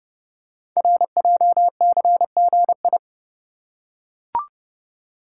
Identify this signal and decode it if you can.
Morse code, heard as a keyed tone, about 31 wpm, 690 Hz. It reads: RJCGI
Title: Code Groups - 5 Letters - 30wpm